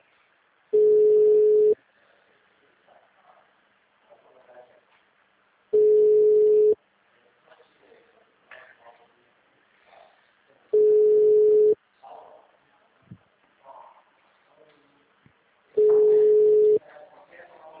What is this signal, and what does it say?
Ringback tone (Europe, 425 Hz), with speech.
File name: out- -2010- - - .313